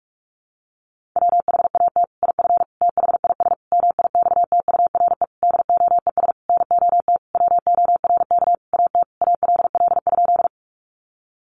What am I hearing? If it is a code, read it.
Morse code, heard as a keyed tone, about 46 wpm, 700 Hz. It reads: WHAT IF THIS MIXTURE DOES NOT WORK AT ALL?
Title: Sentences from Top 1500 Words 45wpm